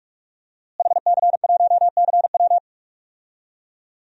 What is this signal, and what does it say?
Morse code, heard as a keyed tone, about 43 wpm, 700 Hz. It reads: HC1CW
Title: DXCC-100 Call Signs - 5 Characters - Repeated 3x - 45wpm